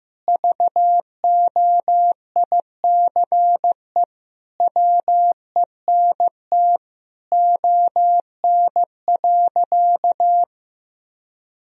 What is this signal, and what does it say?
Morse code, heard as a keyed tone, about 16 wpm, 700 Hz. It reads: VOICE WENT ON.
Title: Sentences from Top 1000 Words - Review Part 1 - 15wpm